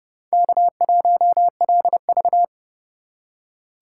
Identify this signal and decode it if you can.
Morse code, heard as a keyed tone, about 29 wpm, 700 Hz. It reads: K1LV